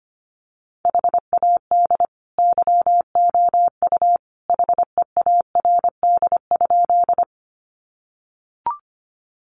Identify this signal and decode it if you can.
Morse code, heard as a keyed tone, about 25 wpm, 700 Hz. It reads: HAD YOU HEARD?
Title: Sentences from Top 400 Words 25wpm